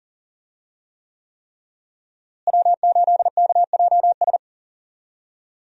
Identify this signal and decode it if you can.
Morse code, heard as a keyed tone, about 37 wpm, 700 Hz. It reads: W8KJS
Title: USA Call Signs - 5 Characters - Repeated 3x - 40wpm